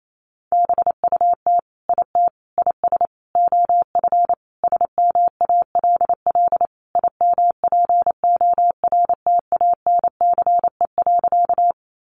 Morse code, heard as a keyed tone, about 28 wpm, 700 Hz. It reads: BUT IT IS OF SMALL IMPORTANCE.
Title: Sentences from Top 1200 Words 28wpm